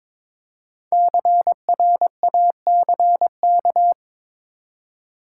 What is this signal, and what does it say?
Morse code, heard as a keyed tone, about 22 wpm, 700 Hz. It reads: CRACK
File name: 5 Letter Words - First 3 Letters Given - 22wpm